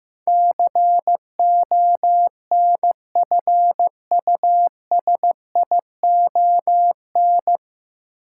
Morse code, heard as a keyed tone, about 15 wpm, 700 Hz. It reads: CONFUSION
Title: Sentences from Top 1600 Words - New Word Drill Alt - 15wpm